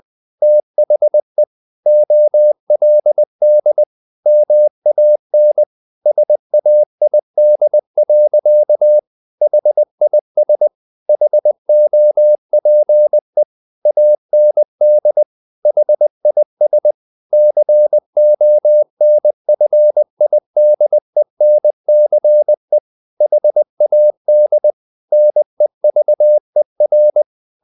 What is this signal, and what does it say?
Morse code, heard as a keyed tone, about 20 wpm, 600 Hz. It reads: THE OLD MAN SAID. HIS HOPE AND HIS CONFIDENCE HAD NEVER
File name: the old man and the sea / the old man and the sea part 1 20wpm